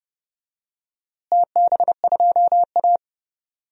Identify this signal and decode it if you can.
Morse code, heard as a keyed tone, about 30 wpm, 700 Hz. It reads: TB2A
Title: DXCC-100 Call Signs - 4 Characters - Repeated 3x - 30wpm